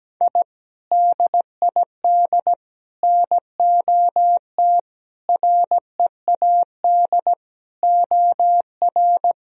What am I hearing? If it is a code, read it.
Morse code, heard as a keyed tone, about 17 wpm, 700 Hz. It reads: I DID NOT READ OR
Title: Sentences from Top 1000 Words - Review Part 3 - 17wpm